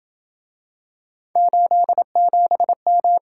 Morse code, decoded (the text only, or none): 87M